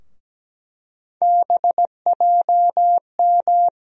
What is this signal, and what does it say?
Morse code, heard as a keyed tone, about 17 wpm, 700 Hz. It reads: BJM
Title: USA Call Sign Postfixes - 3 Letters 17wpm